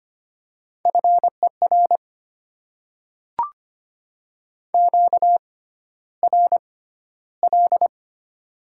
Morse code, heard as a keyed tone, about 25 wpm, spaced as for 13 wpm, 700 Hz. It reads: FER QRL
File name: Top 20 QSO Elements - 6x Character Spacing - 25wpm